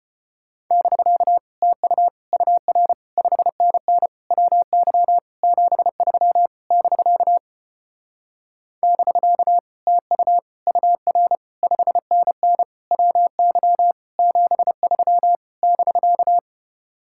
Morse code, heard as a keyed tone, about 33 wpm, 700 Hz. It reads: <BK> TU UR 5NN WY 73 <BK> <BK> TU UR 5NN WY 73 <BK>